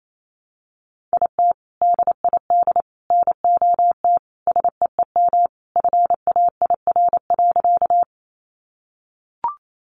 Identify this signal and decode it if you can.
Morse code, heard as a keyed tone, about 28 wpm, 700 Hz. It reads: IT DID NOT SEEM FAIR.
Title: Sentences from Top 1300 Words 28wpm